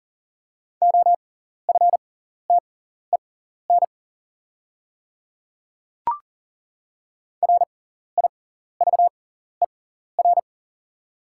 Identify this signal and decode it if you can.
Morse code, heard as a keyed tone, about 40 wpm, spaced as for 14 wpm, 690 Hz. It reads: OFTEN RIVER
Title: 6x Character Spacing - Top 300 Words - 40wpm